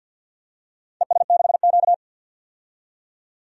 Morse code, heard as a keyed tone, about 48 wpm, 700 Hz. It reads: ES6Q
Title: DXCC-100 Call Signs - 4 Characters - 50wpm